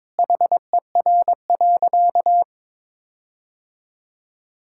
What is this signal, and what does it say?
Morse code, heard as a keyed tone, about 22 wpm, 700 Hz. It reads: HER.
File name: practice / 22wpm / Sentences with 4-Word Phrases - 22wpm